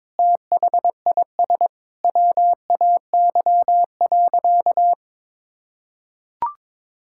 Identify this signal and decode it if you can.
Morse code, heard as a keyed tone, about 22 wpm, 700 Hz. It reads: THIS WAY.
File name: practice / 22wpm / Sentences with 2-Word Phrases - 22wpm